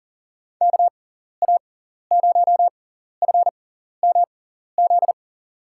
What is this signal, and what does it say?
Morse code, heard as a keyed tone, about 37 wpm, spaced as for 14 wpm, 700 Hz. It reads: KA0FMZ
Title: USA Call Signs - 6 Characters - 6x Character Spacing - 40wpm